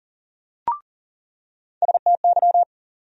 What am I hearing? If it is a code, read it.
Morse code, heard as a keyed tone, about 37 wpm, 700 Hz. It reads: STY
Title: USA Call Sign Postfixes - 3 Letters 40wpm